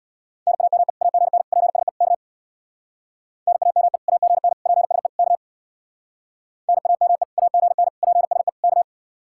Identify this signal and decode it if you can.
Morse code, heard as a keyed tone, about 65 wpm, 700 Hz. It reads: NAME AGN PSE K NAME AGN PSE K NAME AGN PSE K